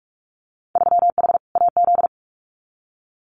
Morse code, heard as a keyed tone, about 44 wpm, 700 Hz. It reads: 3H AZ
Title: Winter Field Day Exchanges Encore 45wpm